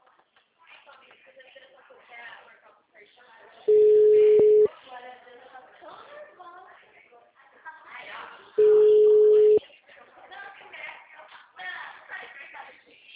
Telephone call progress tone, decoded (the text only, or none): ringback tone